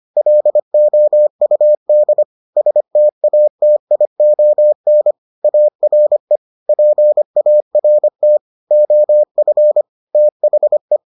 Morse code, heard as a keyed tone, about 25 wpm, 600 Hz. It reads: LOUD STATION ARE PART OF THE